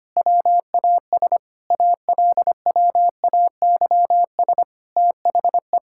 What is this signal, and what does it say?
Morse code, heard as a keyed tone, about 25 wpm, 700 Hz. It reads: WAS ALWAYS THE
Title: Sentences from Top 1000 Words - Review Part 5 - 25wpm